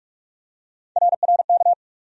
Morse code, heard as a keyed tone, about 41 wpm, 700 Hz. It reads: RRK
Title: USA Call Sign Postfixes - 3 Letters 45wpm